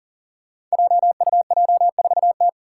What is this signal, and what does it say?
Morse code, heard as a keyed tone, about 40 wpm, 700 Hz. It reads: JUJ4T
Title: Five Challenging Letter-Number Combinations 40wpm